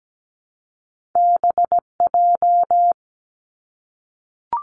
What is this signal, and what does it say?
Morse code, heard as a keyed tone, about 17 wpm, 700 Hz. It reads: BJ
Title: Top Two Letter Combinations in Words 17wpm